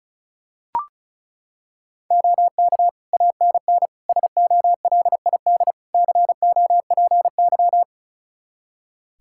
Morse code, heard as a keyed tone, about 34 wpm, 700 Hz. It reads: OK ANN SOLID COPY
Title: K1USN SST QSO Practice 35wpm